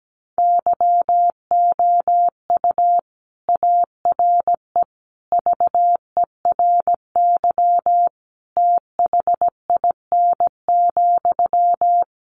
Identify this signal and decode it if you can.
Morse code, heard as a keyed tone, about 17 wpm, 700 Hz. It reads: YOU ARE VERY THIN,
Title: Sentences from Top 1200 Words 17wpm